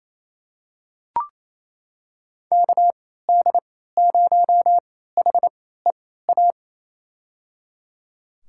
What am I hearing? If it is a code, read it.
Morse code, heard as a keyed tone, about 28 wpm, spaced as for 16 wpm, 700 Hz. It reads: KD0HEA
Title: USA Call Signs - 6 Characters - 3x Character Spacing - 28wpm